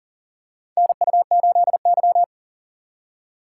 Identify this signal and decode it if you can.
Morse code, heard as a keyed tone, about 40 wpm, 690 Hz. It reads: NU8Y
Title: USA Call Signs - 3x Character Spacing - 40wpm